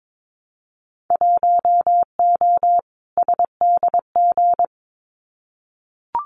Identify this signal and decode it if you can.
Morse code, heard as a keyed tone, about 22 wpm, 700 Hz. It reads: 1O SDG
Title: Winter Field Day Exchanges Encore 22wpm